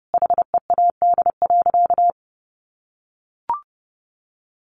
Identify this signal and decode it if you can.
Morse code, heard as a keyed tone, about 30 wpm, 700 Hz. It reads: HEAD.
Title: Sentences from Top 1100 Words 30wpm